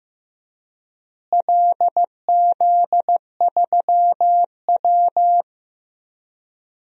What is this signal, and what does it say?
Morse code, heard as a keyed tone, about 15 wpm, 700 Hz. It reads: LZ3W
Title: DXCC-100 Call Signs - 4 Characters - Repeated 3x - 15wpm